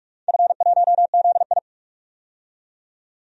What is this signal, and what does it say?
Morse code, heard as a keyed tone, about 44 wpm, 700 Hz. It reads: F1ZI
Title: DXCC-100 Call Signs - 4 Characters - 45wpm